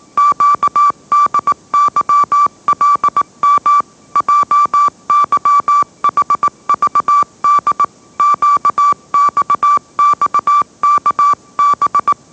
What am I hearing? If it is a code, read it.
Morse code, heard as a keyed tone, about 18 wpm, 1160 Hz. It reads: QDYLM JYHVD QXXKB
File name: mrx / mrx082 20wpm